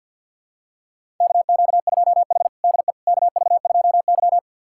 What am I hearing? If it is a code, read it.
Morse code, heard as a keyed tone, about 52 wpm, 700 Hz. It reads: KQ2H DE KV2Y
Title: USA Callsign DE Callsign - 4 Characters - 50wpm